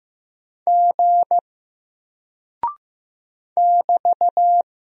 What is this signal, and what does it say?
Morse code, heard as a keyed tone, about 15 wpm, 700 Hz. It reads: G <BT>
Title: Single Character (Letter:Number:Punctuation:Prosign) Alt 15wpm